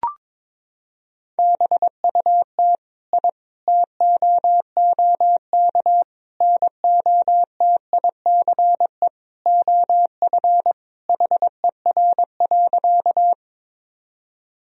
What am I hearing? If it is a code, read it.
Morse code, heard as a keyed tone, about 22 wpm, 700 Hz. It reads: BUT I TOOK NOTICE OF HER.
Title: Sentences from Top 1000 Words - Review Part 2 - 22wpm